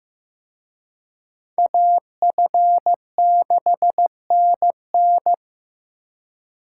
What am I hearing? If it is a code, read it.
Morse code, heard as a keyed tone, about 18 wpm, 700 Hz. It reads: AF6NN
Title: Speed Racing - USA Call Signs 15wpm